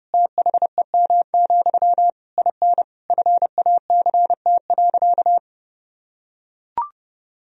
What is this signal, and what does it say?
Morse code, heard as a keyed tone, about 30 wpm, 700 Hz. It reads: THEM, IN FACT.
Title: Sentences from Top 1000 Words - Review Part 3 - 30wpm